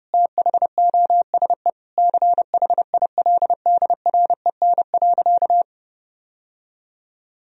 Morse code, heard as a keyed tone, about 30 wpm, 700 Hz. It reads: THOSE CHILDREN.
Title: Sentences from Top 1000 Words - Review Part 1 - 30wpm